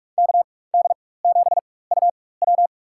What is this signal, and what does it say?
Morse code, heard as a keyed tone, about 46 wpm, spaced as for 21 wpm, 700 Hz. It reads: KD7UW
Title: USA Call Signs - 5 Characters - 4x Character Spacing - 45wpm